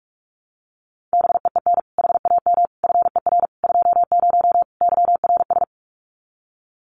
Morse code, heard as a keyed tone, about 44 wpm, 700 Hz. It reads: BEEN HAM FER 20 YRS
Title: Common QSO Phrases 45wpm